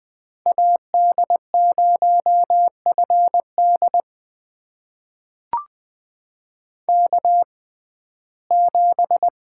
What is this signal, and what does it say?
Morse code, heard as a keyed tone, about 20 wpm, spaced as for 14 wpm, 700 Hz. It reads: AD0FD K7
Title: USA Call Signs - 5 Characters - 6x Character Spacing - 20wpm